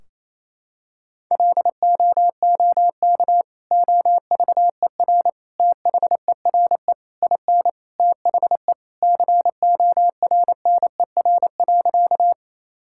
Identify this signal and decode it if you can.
Morse code, heard as a keyed tone, about 28 wpm, 700 Hz. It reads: LOOK OVER THERE IN THE CORNER.
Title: Sentences from Top 1600 Words 28wpm